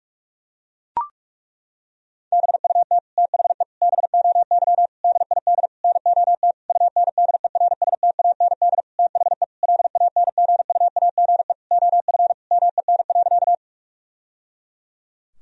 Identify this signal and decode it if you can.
Morse code, heard as a keyed tone, about 41 wpm, 700 Hz. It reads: BUT THE BOY DID NOT UNDERSTAND THE LANGUAGE OF MEN.
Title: Sentences from Top 1000 Words - Review Part 3 - 45wpm